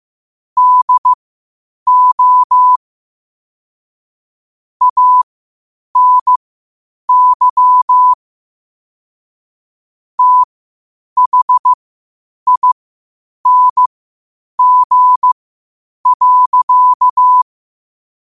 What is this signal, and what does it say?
Morse code, heard as a keyed tone, about 15 wpm, spaced as for 7 wpm, 1000 Hz. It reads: DO ANYTHING.
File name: qotdfiles / 8wpm